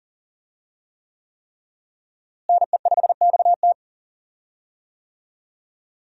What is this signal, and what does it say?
Morse code, heard as a keyed tone, about 40 wpm, 700 Hz. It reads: NE5XT
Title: USA Call Signs - 5 Characters - Repeated 3x - 40wpm